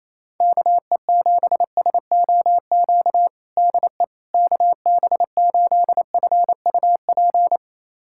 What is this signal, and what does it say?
Morse code, heard as a keyed tone, about 28 wpm, 700 Hz. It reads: KE7SOQ DE KB8FUP
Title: USA Callsign DE Callsign 28wpm